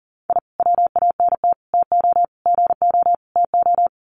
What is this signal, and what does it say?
Morse code, heard as a keyed tone, about 40 wpm, 700 Hz. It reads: I WANT TO GO TO
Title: Sets of 5 Words Encore 40wpm